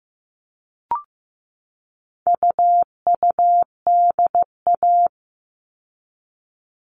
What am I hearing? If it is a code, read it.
Morse code, heard as a keyed tone, about 15 wpm, 700 Hz. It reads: UUDA
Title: All Cut Numbers - 4-Digit 15wpm